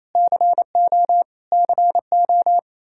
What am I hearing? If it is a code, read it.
Morse code, heard as a keyed tone, about 28 wpm, 700 Hz. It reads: CO CO
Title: Practice QSO - Exchange 1 - Beginning - 28wpm